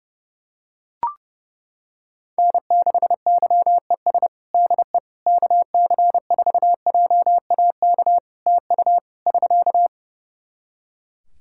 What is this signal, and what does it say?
Morse code, heard as a keyed tone, about 30 wpm, 700 Hz. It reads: N6YES DE KC4JAK TU <SK>
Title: K1USN SST QSO Practice 30wpm